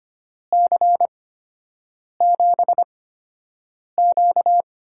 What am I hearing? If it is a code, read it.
Morse code, heard as a keyed tone, about 25 wpm, spaced as for 7 wpm, 700 Hz. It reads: C7Q
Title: USA Call Signs - 5 Characters - 8x Character Spacing - 25wpm